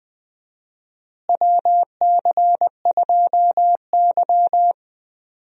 Morse code, heard as a keyed tone, about 20 wpm, 700 Hz. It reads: WC2Y
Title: USA Call Signs - 4 Characters - Repeated 2x - 20wpm